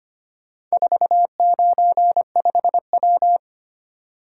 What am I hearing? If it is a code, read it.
Morse code, heard as a keyed tone, about 25 wpm, 700 Hz. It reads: 495W
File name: practice / 25wpm / Four Letter-Number Combinations - 25wpm